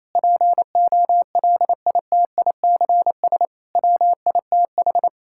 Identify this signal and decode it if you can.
Morse code, heard as a keyed tone, about 28 wpm, 690 Hz. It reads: POLITICS WITH
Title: Sentences from Top 1000 Words - Review Part 6 - 28wpm